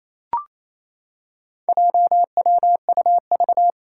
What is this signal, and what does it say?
Morse code, heard as a keyed tone, about 28 wpm, 700 Hz. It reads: JWUV
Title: Four Challenging Letter-Number Combinations 28wpm